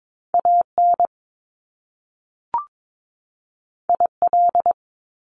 Morse code, heard as a keyed tone, about 22 wpm, 700 Hz. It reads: A N I L
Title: Top Two Letter Combinations in Words 22wpm